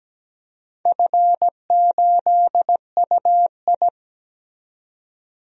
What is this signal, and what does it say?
Morse code, heard as a keyed tone, about 17 wpm, 700 Hz. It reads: F8UI
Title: DXCC-100 Call Signs - 4 Characters - 17wpm